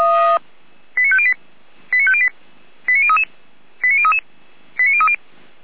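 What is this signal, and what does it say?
Five-tone selective call (ZVEI), heard as quick runs of five tones, tones of about 70 ms. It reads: 89508 89508 89033 89033 89030
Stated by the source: ZVEI five-tone selective call received by SDR.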